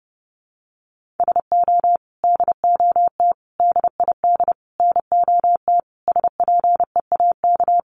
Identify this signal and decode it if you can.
Morse code, heard as a keyed tone, about 30 wpm, 700 Hz. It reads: SO DOT DID NOT SPEAK
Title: Sentences from Top 1500 Words 30wpm